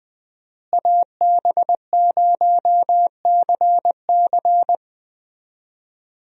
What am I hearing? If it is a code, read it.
Morse code, heard as a keyed tone, about 20 wpm, 700 Hz. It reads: AB0CC
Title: USA Call Signs - 5 Characters - Repeated 2x - 20wpm